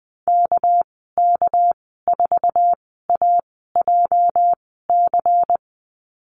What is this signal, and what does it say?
Morse code, heard as a keyed tone, about 20 wpm, spaced as for 14 wpm, 700 Hz. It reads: KK4AJC
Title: 213 Lesson 34g - Number 8 - Call Signs 20wpm